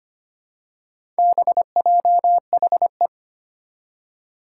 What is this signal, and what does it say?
Morse code, heard as a keyed tone, about 25 wpm, 700 Hz. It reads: BJHE
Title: Four Challenging Letter-Number Combinations 25wpm